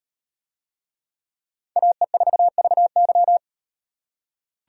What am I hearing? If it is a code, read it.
Morse code, heard as a keyed tone, about 36 wpm, 700 Hz. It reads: AE4VY